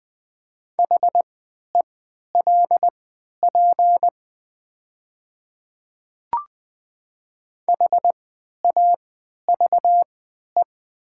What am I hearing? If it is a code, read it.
Morse code, heard as a keyed tone, about 20 wpm, spaced as for 11 wpm, 700 Hz. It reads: HELP HAVE
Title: 3x Character Spacing - Top 200 Words - 20wpm